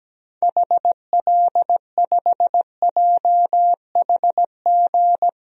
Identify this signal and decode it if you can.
Morse code, heard as a keyed tone, about 17 wpm, 700 Hz. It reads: HL5JHG
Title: DXCC-100 Call Signs - 6 Characters - 17wpm